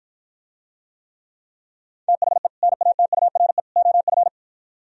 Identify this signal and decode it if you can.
Morse code, heard as a keyed tone, about 54 wpm, 700 Hz. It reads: THE NATURE OF